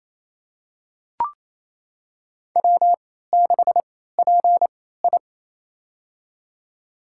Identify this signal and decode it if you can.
Morse code, heard as a keyed tone, about 27 wpm, spaced as for 16 wpm, 700 Hz. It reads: W6PI
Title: USA Call Signs - 3x Character Spacing - 28wpm